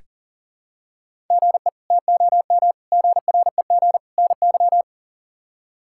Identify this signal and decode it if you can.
Morse code, heard as a keyed tone, about 40 wpm, 700 Hz. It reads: GE TOM GREG NY
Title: K1USN SST QSO Practice - 40wpm